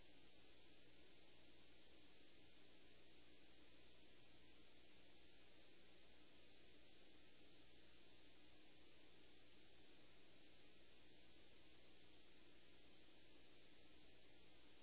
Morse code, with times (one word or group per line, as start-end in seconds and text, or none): none